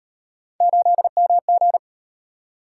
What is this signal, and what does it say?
Morse code, heard as a keyed tone, about 36 wpm, 700 Hz. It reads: 8MG